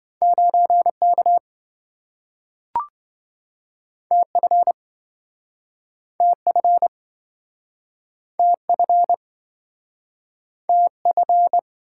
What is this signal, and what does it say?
Morse code, heard as a keyed tone, about 27 wpm, 700 Hz. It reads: 9K TF TF TF TF